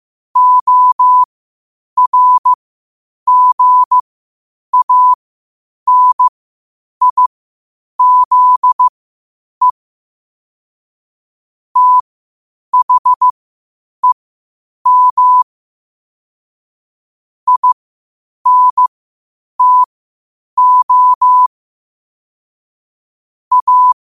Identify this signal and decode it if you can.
Morse code, heard as a keyed tone, about 15 wpm, spaced as for 8 wpm, 1000 Hz. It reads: ORGANIZE THEM INTO A
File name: qotdfiles / 8wpm